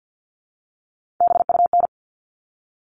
Morse code, heard as a keyed tone, about 48 wpm, 700 Hz. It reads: BUN
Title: All Cut Numbers - 3-Digit 50wpm